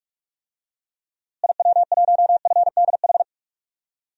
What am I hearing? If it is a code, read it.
Morse code, heard as a keyed tone, about 46 wpm, 700 Hz. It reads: IW1FDH
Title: DXCC-100 Call Signs - 6 Characters - Repeated 3x - 45wpm